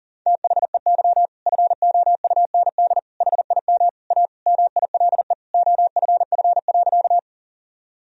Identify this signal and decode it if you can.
Morse code, heard as a keyed tone, about 40 wpm, 700 Hz. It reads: THEY FOUND HIM A MILE OFF.
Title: Sentences from Top 1000 Words - Review Part 1 - 40wpm